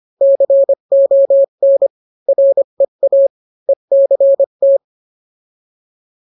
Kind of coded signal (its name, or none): Morse code